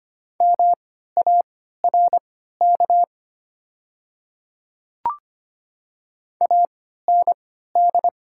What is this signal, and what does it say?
Morse code, heard as a keyed tone, about 25 wpm, spaced as for 14 wpm, 700 Hz. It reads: MARK AND
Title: 3x Character Spacing - Top 500 Words - 25wpm